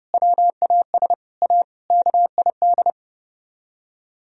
Morse code, heard as a keyed tone, about 37 wpm, 700 Hz. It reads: WAS A KID
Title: Speed-Racing - Sets of 5 Words 30wpm